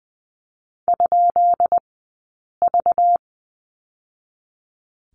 Morse code, heard as a keyed tone, about 20 wpm, spaced as for 8 wpm, 700 Hz. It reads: ?V
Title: 190 Lesson 31d - Letter v - Two Characters 20wpm